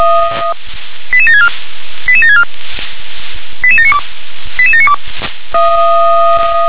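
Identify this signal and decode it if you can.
Five-tone selective call (ZVEI), heard as quick runs of five tones, tones of about 70 ms. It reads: 88764 88764 88792 88792
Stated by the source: ZVEI five-tone selective call received on SDR.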